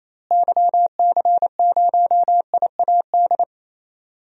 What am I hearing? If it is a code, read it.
Morse code, heard as a keyed tone, about 28 wpm, 700 Hz. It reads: YC0IAD
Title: DXCC-100 Call Signs - 6 Characters - Repeated 2x - 28wpm